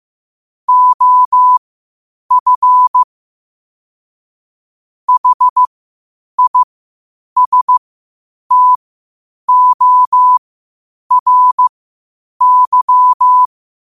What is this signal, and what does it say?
Morse code, heard as a keyed tone, about 15 wpm, spaced as for 8 wpm, 1000 Hz. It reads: OF HISTORY
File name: qotdfiles / 8wpm